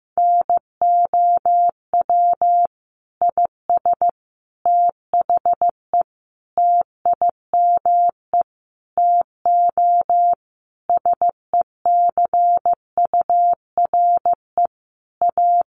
Morse code, heard as a keyed tone, about 15 wpm, 700 Hz. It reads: NOW IS THE TIME TO SECURE A
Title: Sentences from Top 1200 Words 15wpm